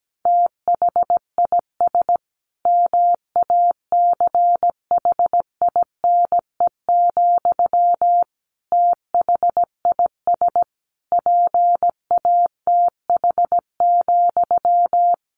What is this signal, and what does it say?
Morse code, heard as a keyed tone, about 17 wpm, 700 Hz. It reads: THIS MACHINE, THIS PATH,
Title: A Sound Of Thunder 17wpm